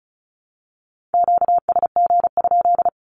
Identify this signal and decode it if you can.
Morse code, heard as a keyed tone, about 34 wpm, 700 Hz. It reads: QSG?
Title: Q-Codes 35wpm